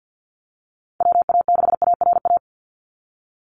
Morse code, heard as a keyed tone, about 50 wpm, 700 Hz. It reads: WA6ARA
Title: NA SOTA Chaser Callsigns 50wpm